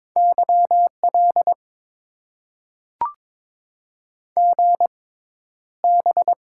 Morse code, heard as a keyed tone, about 22 wpm, spaced as for 11 wpm, 700 Hz. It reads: YL GB
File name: Top 90 QSO Elements - 6x Character Spacing - 22wpm